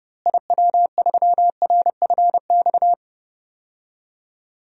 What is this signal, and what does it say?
Morse code, heard as a keyed tone, about 30 wpm, 700 Hz. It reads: IW3RFX
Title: DXCC-100 Call Signs - 6 Characters - Repeated 2x - 30wpm